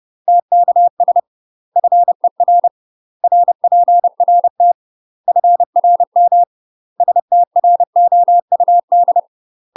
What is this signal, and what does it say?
Morse code, heard as a keyed tone, about 30 wpm, 700 Hz. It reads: TKS FER RPRT FRM STROUD
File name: UK MASTER QSOs TRAINING / 30WPM / UK MASTER QSOs 0011